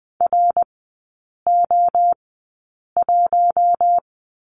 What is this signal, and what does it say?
Morse code, heard as a keyed tone, about 20 wpm, spaced as for 8 wpm, 700 Hz. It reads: RO1
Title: 093 Lesson 16e - Letter c - Three Characters 20wpm